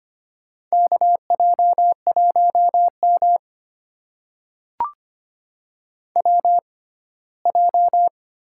Morse code, heard as a keyed tone, about 25 wpm, spaced as for 17 wpm, 700 Hz. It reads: KJ1M WJ
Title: USA Call Signs - 4 Characters - 6x Character Spacing - 25wpm